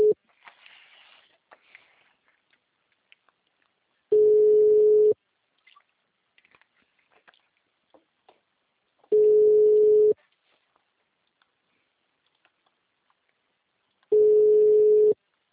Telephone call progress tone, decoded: ringback tone

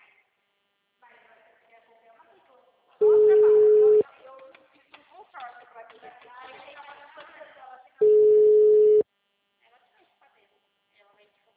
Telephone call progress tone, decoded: ringback tone

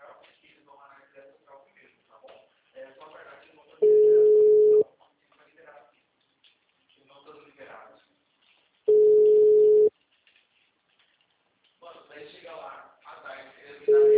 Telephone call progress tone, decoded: ringback tone

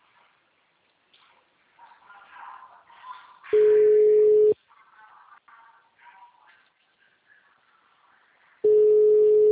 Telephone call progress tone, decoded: ringback tone

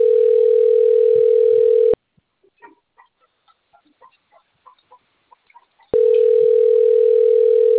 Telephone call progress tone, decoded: ringback tone

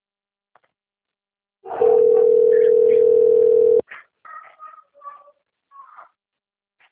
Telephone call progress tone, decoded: ringback tone

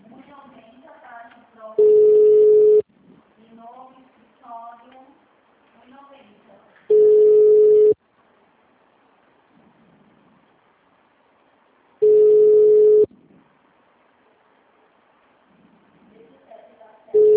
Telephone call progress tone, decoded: ringback tone

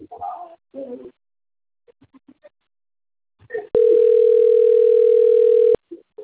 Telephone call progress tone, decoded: ringback tone

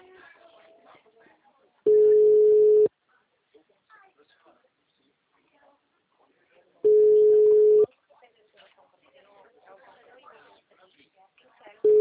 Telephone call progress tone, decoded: ringback tone

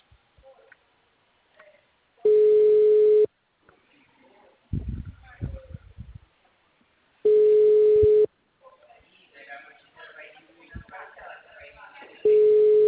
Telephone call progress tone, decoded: ringback tone

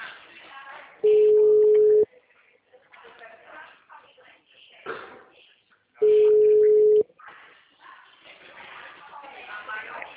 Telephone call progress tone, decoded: ringback tone

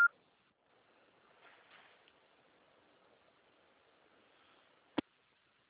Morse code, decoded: none

